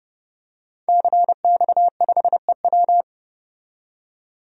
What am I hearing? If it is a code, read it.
Morse code, heard as a keyed tone, about 30 wpm, 700 Hz. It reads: CX5EW
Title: DXCC-100 Call Signs - 5 Characters - 30wpm